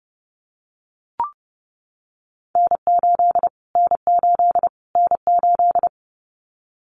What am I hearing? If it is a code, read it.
Morse code, heard as a keyed tone, about 30 wpm, 700 Hz. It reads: N8 N8 N8
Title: Callsign Prefixes for Top 25 DXCC - Repeated 3x - 30wpm